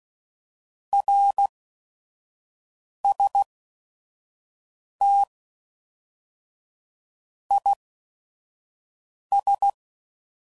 Morse code, heard as a keyed tone, about 16 wpm, spaced as for 5 wpm, 800 Hz. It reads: RST IS
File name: CW Code Groups @5WPM / QSOs / QSO WA7QOX 1